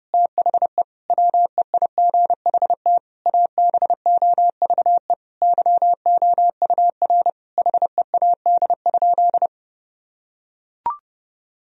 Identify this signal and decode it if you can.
Morse code, heard as a keyed tone, about 32 wpm, 700 Hz. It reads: THE WEIGHT ABOVE YOUR HEAD?
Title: Sentences from Top 1000 Words - Review Part 3 - 30wpm